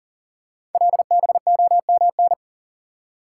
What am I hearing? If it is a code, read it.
Morse code, heard as a keyed tone, about 38 wpm, 700 Hz. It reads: LBOMN
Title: Code Groups - 5 Letters - 40wpm